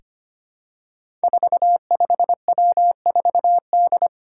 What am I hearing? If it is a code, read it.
Morse code, heard as a keyed tone, about 25 wpm, 700 Hz. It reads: 45W4D
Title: Five Challenging Letter-Number Combinations 25wpm